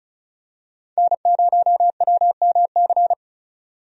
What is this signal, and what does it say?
Morse code, heard as a keyed tone, about 36 wpm, 700 Hz. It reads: N0WMC
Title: USA Call Signs - 5 Characters - 35wpm